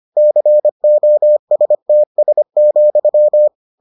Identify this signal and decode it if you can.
Morse code, heard as a keyed tone, about 25 wpm, 600 Hz. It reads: COSTS,